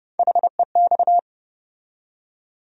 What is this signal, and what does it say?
Morse code, heard as a keyed tone, about 30 wpm, 700 Hz. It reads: HEX
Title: USA Call Sign Postfixes - 3 Letters 30wpm